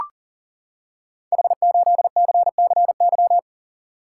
Morse code, heard as a keyed tone, about 40 wpm, 700 Hz. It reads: H8CCY
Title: Code Groups - 5 Letter:Numbers - 40wpm